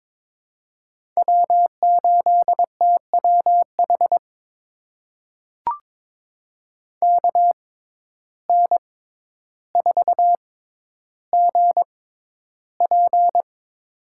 Morse code, heard as a keyed tone, about 22 wpm, spaced as for 11 wpm, 700 Hz. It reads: W8TWH KN4GP
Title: USA Call Signs - 5 Characters - 6x Character Spacing - 22wpm